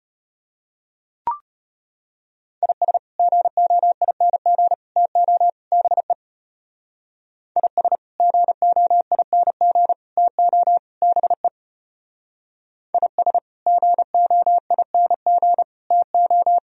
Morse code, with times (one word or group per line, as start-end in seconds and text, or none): 2.6-3.0 IS
3.2-4.7 GOING
5.0-5.5 TO
5.7-6.1 BE
7.6-8.0 IS
8.2-9.9 GOING
10.2-10.8 TO
11.0-11.5 BE
12.9-13.4 IS
13.7-15.6 GOING
15.9-16.6 TO